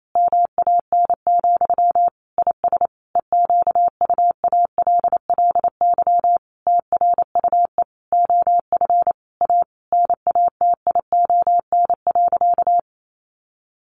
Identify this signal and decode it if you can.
Morse code, heard as a keyed tone, about 28 wpm, 700 Hz. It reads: MAN, IS EQUALLY TRUE OF A NATION.
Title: Sentences from Top 1400 Words 28wpm